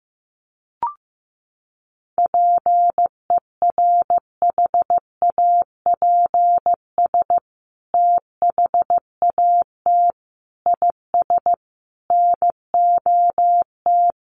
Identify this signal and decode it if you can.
Morse code, heard as a keyed tone, about 15 wpm, 700 Hz. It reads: PERHAPS THAT IS NOT
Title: Sentences from Top 900 Words 15wpm